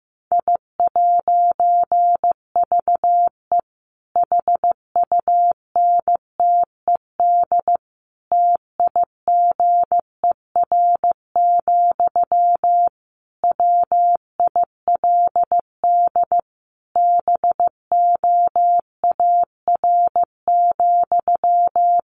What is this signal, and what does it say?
Morse code, heard as a keyed tone, about 15 wpm, 700 Hz. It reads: I'VE HUNTED TIGER, WILD BOAR,
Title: A Sound Of Thunder 15wpm